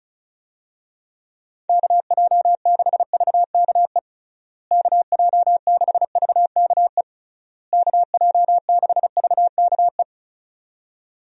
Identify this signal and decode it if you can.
Morse code, heard as a keyed tone, about 34 wpm, 700 Hz. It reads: KJ6VKE KJ6VKE KJ6VKE
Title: USA Call Signs - 6 Characters - Repeated 3x - 35wpm